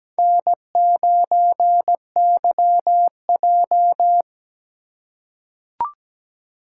Morse code, heard as a keyed tone, about 17 wpm, 700 Hz. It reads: N9YJ
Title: USA Call Signs - 4 Characters - 4x Character Spacing - 17wpm